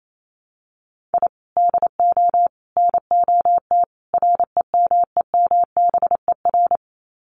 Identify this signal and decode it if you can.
Morse code, heard as a keyed tone, about 28 wpm, 700 Hz. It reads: I DO NOT REMEMBER
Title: Sets of 4 Words 28wpm